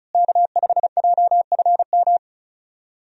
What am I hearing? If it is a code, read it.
Morse code, heard as a keyed tone, about 34 wpm, 700 Hz. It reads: K5JFM
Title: USA Call Signs - 5 Characters - Repeated 3x - 35wpm